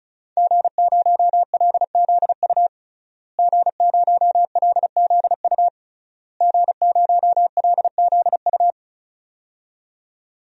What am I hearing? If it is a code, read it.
Morse code, heard as a keyed tone, about 36 wpm, 700 Hz. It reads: G0LZU G0LZU G0LZU
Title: DXCC-100 Call Signs - 5 Characters - Repeated 3x - 35wpm